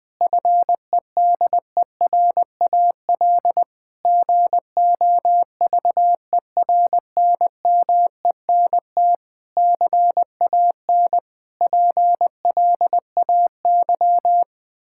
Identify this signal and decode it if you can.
Morse code, heard as a keyed tone, about 21 wpm, 700 Hz. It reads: FEDERAL GOVERNMENT CAN PLAY
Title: Sentences from Top 1000 Words - Review Part 5 - 20wpm